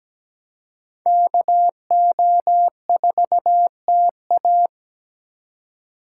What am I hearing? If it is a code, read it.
Morse code, heard as a keyed tone, about 20 wpm, 700 Hz. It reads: KO4TA
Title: Speed Racing - USA 2x2 Call Signs 17wpm